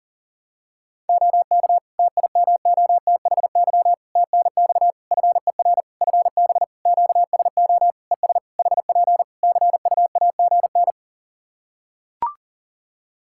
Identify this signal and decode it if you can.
Morse code, heard as a keyed tone, about 40 wpm, 700 Hz. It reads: OK TIMOTHY TNX FER FB QSO ES HP CUAGN
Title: K1USN SST QSO Practice 40wpm